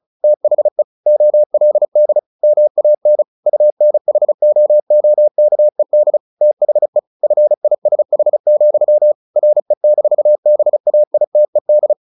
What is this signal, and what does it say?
Morse code, heard as a keyed tone, about 34 wpm, 600 Hz. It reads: THE OLD MAN UNHOOKED THE FISH, RE-BAITED
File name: the old man and the sea 2 35wpm